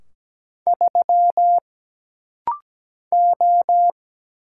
Morse code, heard as a keyed tone, about 17 wpm, 700 Hz. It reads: 3 O
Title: Single Letter-Number - Rapid-Fire 17wpm